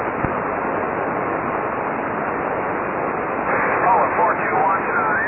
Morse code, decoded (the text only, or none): none